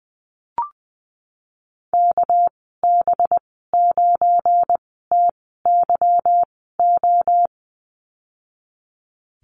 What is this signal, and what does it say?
Morse code, heard as a keyed tone, about 19 wpm, spaced as for 14 wpm, 700 Hz. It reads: KB9TYO
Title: 238 Lesson 38g - Letter q - Call Signs 20wpm